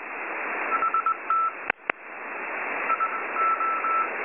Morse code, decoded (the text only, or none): HT I O